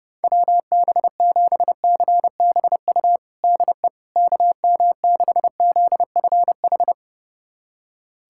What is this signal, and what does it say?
Morse code, heard as a keyed tone, about 30 wpm, 700 Hz. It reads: WB7CBU DE KM6ZFH
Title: USA Callsign DE Callsign - 6 Characters - 30wpm